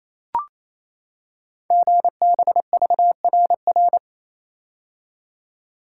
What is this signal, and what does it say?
Morse code, heard as a keyed tone, about 29 wpm, 700 Hz. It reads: GBVRR
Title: Code Groups - 5 Letters - 28wpm